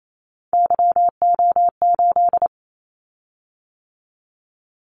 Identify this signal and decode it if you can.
Morse code, heard as a keyed tone, about 28 wpm, 700 Hz. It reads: YO8
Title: Callsign Prefixes for Top 25 DXCC - 28wpm